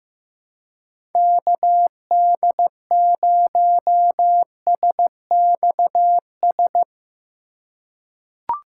Morse code, heard as a keyed tone, about 15 wpm, 700 Hz. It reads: KD0SXS
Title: USA Call Signs - 6 Characters - 3x Character Spacing - 15wpm